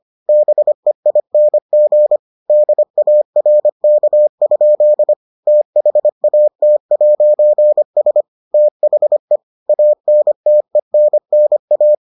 Morse code, heard as a keyed tone, about 25 wpm, 600 Hz. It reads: BEING DARK? THAT'S THE ANTENNA